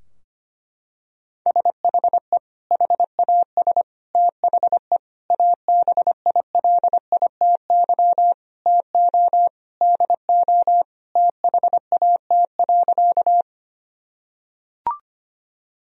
Morse code, heard as a keyed tone, about 25 wpm, 700 Hz. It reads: SHE HAS THE ABILITY TO DO THAT.
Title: Sentences from Top 1000 Words - Review Part 3 - 25wpm